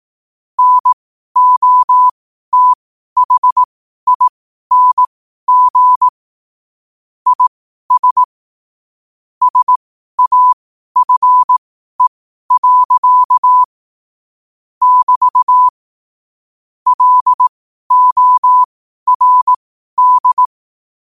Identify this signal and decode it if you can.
Morse code, heard as a keyed tone, about 18 wpm, spaced as for 12 wpm, 1000 Hz. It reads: NOTHING IS SAFE. <BT> LORD
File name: qotdfiles / 12wpm